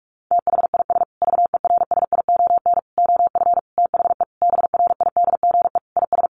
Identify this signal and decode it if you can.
Morse code, heard as a keyed tone, about 44 wpm, 700 Hz. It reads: THIS VERSION OF THE BRIDGE IS
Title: Sentences from Top 1400 Words 45wpm